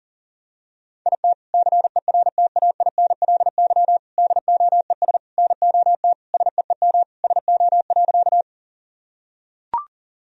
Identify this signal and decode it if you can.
Morse code, heard as a keyed tone, about 37 wpm, 700 Hz. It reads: IT CERTAINLY DOES NOT SEEM SO.
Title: Sentences from Top 1000 Words - Review Part 1 - 40wpm